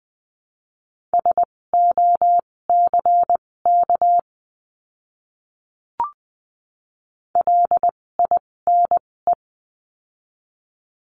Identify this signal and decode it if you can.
Morse code, heard as a keyed tone, about 19 wpm, spaced as for 16 wpm, 700 Hz. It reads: SOCK LINE
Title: 256 Lesson 42a - Words 12-20wpm